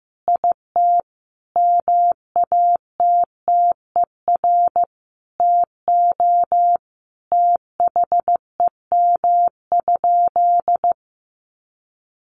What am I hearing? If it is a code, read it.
Morse code, heard as a keyed tone, about 15 wpm, 700 Hz. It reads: IT MATTER TO THEM?
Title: Sentences from Top 700 Words 15wpm